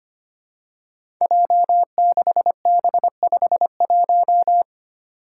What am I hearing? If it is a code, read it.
Morse code, heard as a keyed tone, about 25 wpm, 700 Hz. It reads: J6B51
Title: Five Challenging Letter-Number Combinations 25wpm